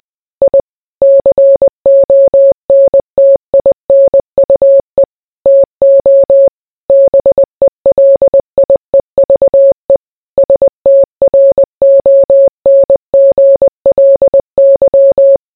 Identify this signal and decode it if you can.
Morse code, heard as a keyed tone, about 20 wpm, 550 Hz. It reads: I CONTINUE TO BELIEVE STRONGLY